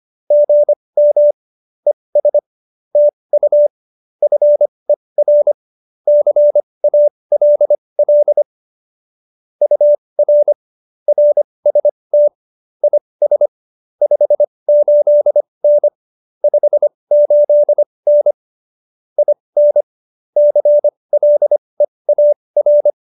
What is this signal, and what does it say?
Morse code, heard as a keyed tone, about 25 wpm, spaced as for 20 wpm, 590 Hz. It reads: GM ES TU FER CALL UR RST IS 58N 58N IN CLEAR